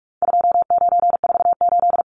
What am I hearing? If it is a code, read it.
Morse code, heard as a keyed tone, about 44 wpm, 700 Hz. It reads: 2948
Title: Numbers - 4-Digit 45wpm